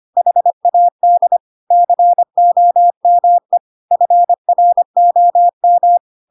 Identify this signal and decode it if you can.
Morse code, heard as a keyed tone, about 25 wpm, 700 Hz. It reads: HAD COME FROM
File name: Pinocchio / Pinocchio ch1 25 25 ik1hge